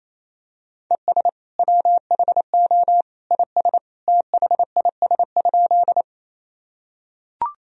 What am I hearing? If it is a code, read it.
Morse code, heard as a keyed tone, about 28 wpm, 700 Hz. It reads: ES WHO IS THIS?
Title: Sentences from Top 100 Words and Abbreviations 28wpm